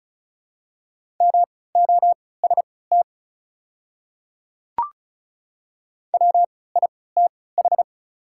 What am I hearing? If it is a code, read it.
Morse code, heard as a keyed tone, about 36 wpm, spaced as for 20 wpm, 690 Hz. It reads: MOST WITH
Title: 3x Character Spacing - Top 100 Words - 35wpm